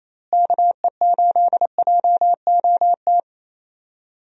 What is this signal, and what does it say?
Morse code, heard as a keyed tone, about 28 wpm, 700 Hz. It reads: KE8JOT
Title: USA Call Signs - 6 Characters - Repeated 3x - 28wpm